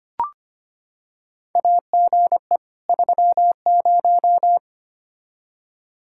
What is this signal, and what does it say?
Morse code, heard as a keyed tone, about 25 wpm, 700 Hz. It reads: AGE 30
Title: Practice QSO - Exchange 3 - More Info - 25wpm